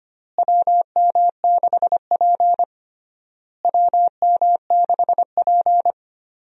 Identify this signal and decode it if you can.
Morse code, heard as a keyed tone, about 25 wpm, 700 Hz. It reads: WM6P WM6P
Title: USA Call Signs - 4 Characters - Repeated 3x - 25wpm